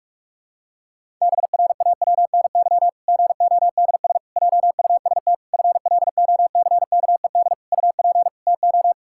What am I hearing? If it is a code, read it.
Morse code, heard as a keyed tone, about 48 wpm, 700 Hz. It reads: BRAWNY GODS JUST FLOCKED UP TO
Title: Pangram Challenge v2 - 45wpm